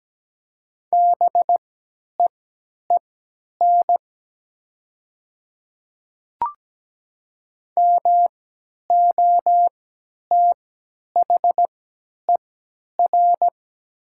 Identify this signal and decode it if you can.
Morse code, heard as a keyed tone, about 17 wpm, spaced as for 10 wpm, 700 Hz. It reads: BEEN MOTHER
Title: 3x Character Spacing - Top 200 Words - 17wpm